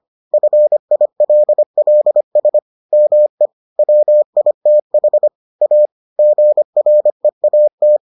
Morse code, heard as a keyed tone, about 25 wpm, 600 Hz. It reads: FILLS ME WITH A GREAT